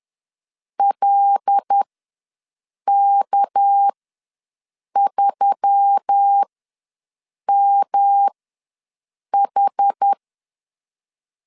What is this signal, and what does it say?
Morse code, heard as a keyed tone, about 10 wpm, spaced as for 6 wpm, 800 Hz. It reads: LK3MH
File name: LEZIONI / 6 PKX0 / 6 PK ws 10wpm 3745